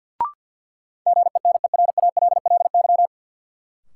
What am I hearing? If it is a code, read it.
Morse code, heard as a keyed tone, about 52 wpm, 700 Hz. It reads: GENERALLY
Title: Sentences from Top 600 Words - New Word Drill - 50wpm